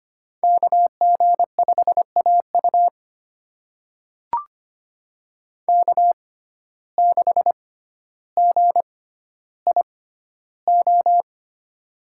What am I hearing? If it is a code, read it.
Morse code, heard as a keyed tone, about 25 wpm, spaced as for 13 wpm, 700 Hz. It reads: KG5AU K6GIO
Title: USA Call Signs - 6x Character Spacing - 25wpm